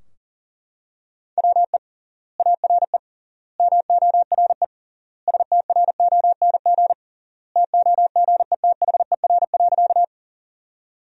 Morse code, heard as a keyed tone, about 38 wpm, 700 Hz. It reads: WE ARE MORE STRONG TOGETHER.
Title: 3x Word-Spacing - Sentences from Top 500 Words - 40wpm